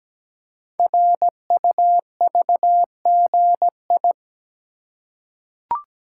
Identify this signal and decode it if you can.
Morse code, heard as a keyed tone, about 17 wpm, 700 Hz. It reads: RUVGI
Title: Code Groups - 5 Letter:Numbers - 17wpm